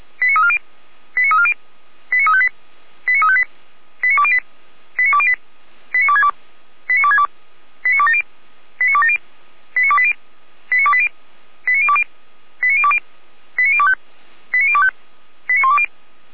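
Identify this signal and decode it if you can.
Five-tone selective call (ZVEI), heard as quick runs of five tones, tones of about 70 ms. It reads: 89340 89340 89358 89358 89208 89208 89262 89262 89270 89270 89280 89280 89020 89020 89025 89025 89120
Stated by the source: ZVEI five-tone selective call received on an SDR, from Feuerwehr Orlamünde.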